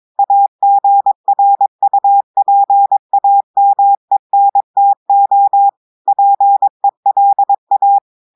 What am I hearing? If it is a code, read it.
Morse code, heard as a keyed tone, about 22 wpm, 800 Hz. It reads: AGRUPAMENTO PELA
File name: audios / portugais / portugais texte en clair 1 vitesse 22 mots minutes